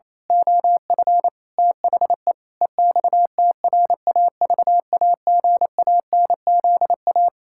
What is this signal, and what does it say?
Morse code, heard as a keyed tone, about 27 wpm, 700 Hz. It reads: OF THE EXTRAVAGANZA
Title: Pangram Challenge v2 - 28wpm